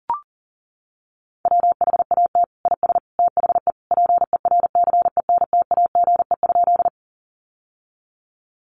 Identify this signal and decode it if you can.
Morse code, heard as a keyed tone, about 40 wpm, 700 Hz. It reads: WHAT IS THE PERCENTAGE?
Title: Sentences from Top 1300 Words 40wpm